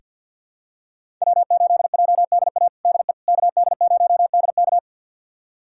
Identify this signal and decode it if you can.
Morse code, heard as a keyed tone, about 48 wpm, 700 Hz. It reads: W8JDA DE KD0DK
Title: USA Callsign DE Callsign - 5 Characters - 50wpm